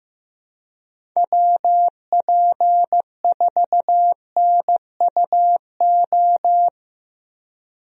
Morse code, heard as a keyed tone, about 15 wpm, 700 Hz. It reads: WP4NUO
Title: USA Call Signs - 6 Characters - Repeated 3x - 15wpm